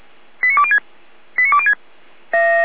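Five-tone selective call (ZVEI), heard as quick runs of five tones, tones of about 80 ms. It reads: 89206 89206
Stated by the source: ZVEI five-tone selective call received by SDR, from the feuerwehr-orlamuende (Orlamünde fire brigade) server.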